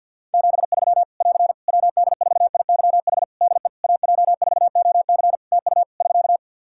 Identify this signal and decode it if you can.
Morse code, heard as a keyed tone, about 47 wpm, 700 Hz. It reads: 73 <AR> WD4IYH DE AJ4OC TU <SK>
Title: Practice QSO - Exchange 4 - Ending - 50wpm